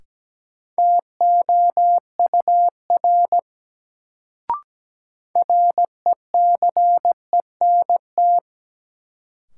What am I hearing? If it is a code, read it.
Morse code, heard as a keyed tone, about 17 wpm, 700 Hz. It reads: TOUR RECENT
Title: Sentences from Top 1400 Words - New Word Drill Alt - 17wpm